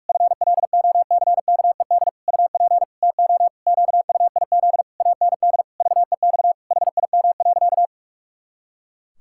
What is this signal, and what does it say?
Morse code, heard as a keyed tone, about 46 wpm, 700 Hz. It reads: FLOCKED UP TO QUIZ AND VEX HIM.
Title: Pangram Challenge v2 - 45wpm